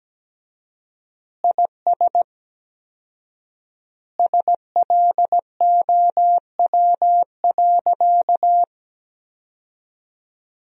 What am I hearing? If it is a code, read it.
Morse code, heard as a keyed tone, about 17 wpm, 700 Hz. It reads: IS SLOW.
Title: 4x Word-Spacing - Sentences from Top 400 Words - 17wpm